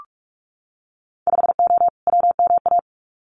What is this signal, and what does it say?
Morse code, heard as a keyed tone, about 46 wpm, 700 Hz. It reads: 5O WMA
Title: Winter Field Day Exchanges 45wpm